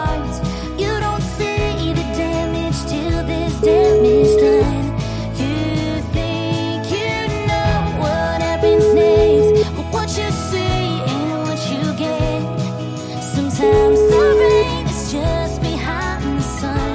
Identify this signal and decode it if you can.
Ringback tone (Europe, 425 Hz), with noise.